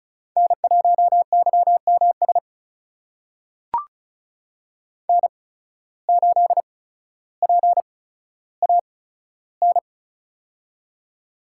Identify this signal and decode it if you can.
Morse code, heard as a keyed tone, about 34 wpm, spaced as for 15 wpm, 700 Hz. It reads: N1YMS N8PAN
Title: USA Call Signs - 5 Characters - 8x Character Spacing - 35wpm